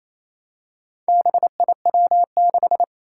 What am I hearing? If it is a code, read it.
Morse code, heard as a keyed tone, about 29 wpm, 700 Hz. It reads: BIW6
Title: Four Challenging Letter-Number Combinations 28wpm